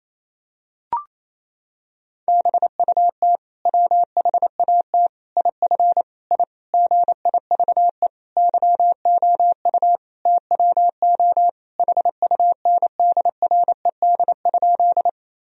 Morse code, heard as a keyed tone, about 27 wpm, 700 Hz. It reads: BUT WHAT IF I GIVE YOU TWO HUNDRED?
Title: Sentences from Top 1000 Words - Review Part 2 - 28wpm